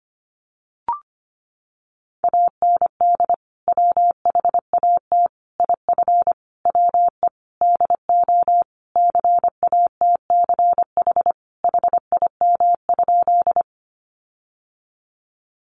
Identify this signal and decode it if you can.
Morse code, heard as a keyed tone, about 25 wpm, 700 Hz. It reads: AND WHAT IF WE DO CATCH HIM?
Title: Sentences from Top 1500 Words 25wpm